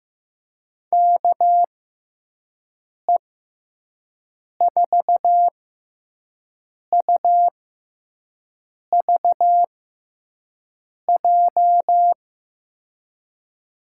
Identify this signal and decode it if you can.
Morse code, heard as a keyed tone, about 15 wpm, spaced as for 5 wpm, 700 Hz. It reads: KE4UVJ
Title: USA Call Signs - 6 Characters - 6x Character Spacing - 15wpm